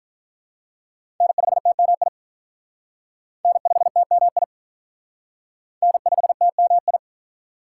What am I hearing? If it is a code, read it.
Morse code, heard as a keyed tone, about 44 wpm, 700 Hz. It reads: N5TMI N5TMI N5TMI
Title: Speed Racing - USA Call Signs 35wpm